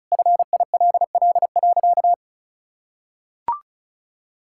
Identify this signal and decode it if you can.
Morse code, heard as a keyed tone, about 36 wpm, 700 Hz. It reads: FILL.
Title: Sentences from Top 1000 Words - Review Part 5 - 35wpm